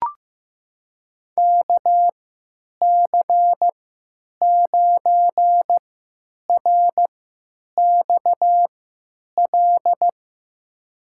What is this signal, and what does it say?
Morse code, heard as a keyed tone, about 15 wpm, spaced as for 9 wpm, 700 Hz. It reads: KC9RXL
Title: USA Call Signs - 6 Characters - 3x Character Spacing - 15wpm